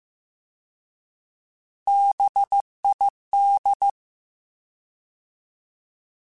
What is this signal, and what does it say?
Morse code, heard as a keyed tone, about 15 wpm, 800 Hz. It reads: BID
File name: Alphabetical / Alphabetical mp3s / Bs @13